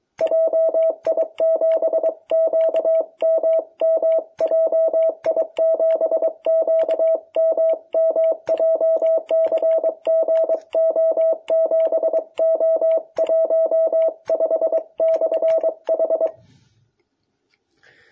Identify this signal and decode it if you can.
Morse code, heard as a keyed tone, about 23 wpm, 660 Hz. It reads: JI7QMM JI7QMM JCGO7O15/H
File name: jh8cbh / 2021cw